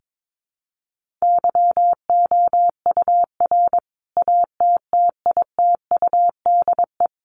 Morse code, heard as a keyed tone, about 22 wpm, 700 Hz. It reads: YOUR ATTITUDE
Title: Sentences from Top 1300 Words 22wpm